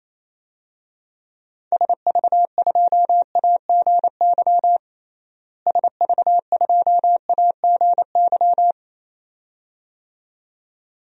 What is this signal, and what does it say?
Morse code, heard as a keyed tone, about 28 wpm, 700 Hz. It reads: SV2AGY SV2AGY
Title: DXCC-100 Call Signs - 6 Characters - Repeated 2x - 28wpm